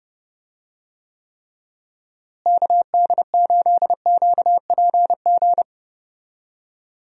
Morse code, heard as a keyed tone, about 30 wpm, 700 Hz. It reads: KD8QPG
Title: USA Call Signs - 6 Characters - Repeated 3x - 30wpm